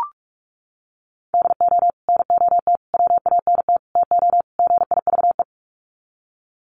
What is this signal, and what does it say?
Morse code, heard as a keyed tone, about 44 wpm, 700 Hz. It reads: DO NOT WANT TO GIVE
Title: Sets of 5 Words Encore 45wpm